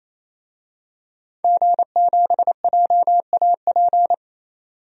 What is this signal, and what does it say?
Morse code, heard as a keyed tone, about 28 wpm, 700 Hz. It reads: G7JAP
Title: DXCC-100 Call Signs - 5 Characters - 28wpm